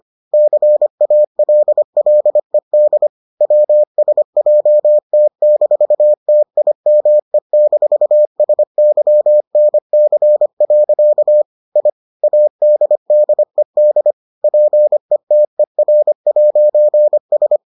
Morse code, heard as a keyed tone, about 25 wpm, 600 Hz. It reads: CALLED WSJT-TIME-SYNC. I ADDED PETER'S